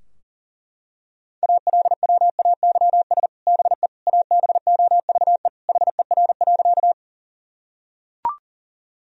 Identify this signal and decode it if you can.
Morse code, heard as a keyed tone, about 37 wpm, 700 Hz. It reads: ALWAYS BE ABOVE HER.
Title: Sentences from Top 1000 Words - Review Part 2 - 40wpm